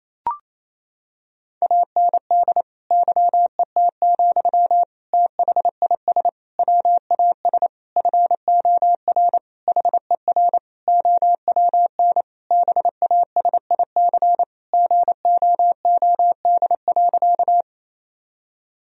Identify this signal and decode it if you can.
Morse code, heard as a keyed tone, about 28 wpm, 700 Hz. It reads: AND YET, THIS WAS FOR HER OWN BASIC GOOD.
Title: Sentences from Top 1000 Words - Review Part 3 - 28wpm